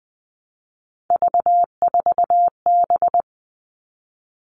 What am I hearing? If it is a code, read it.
Morse code, heard as a keyed tone, about 20 wpm, 700 Hz. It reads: V4B
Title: All Cut Numbers - 3-Digit 20wpm